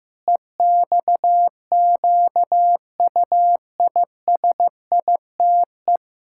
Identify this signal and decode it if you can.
Morse code, heard as a keyed tone, about 14 wpm, 700 Hz. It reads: EXQUISITE
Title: Pangram Challenge v2 - 15wpm